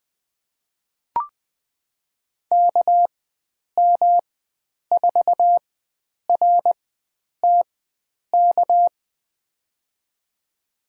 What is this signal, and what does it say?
Morse code, heard as a keyed tone, about 20 wpm, spaced as for 9 wpm, 700 Hz. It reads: KM4RTK
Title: USA Call Signs - 6 Characters - 4x Character Spacing - 20wpm